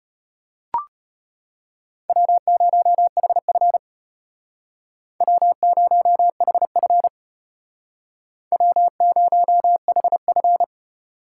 Morse code, heard as a keyed tone, about 32 wpm, 700 Hz. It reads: W0HF W0HF W0HF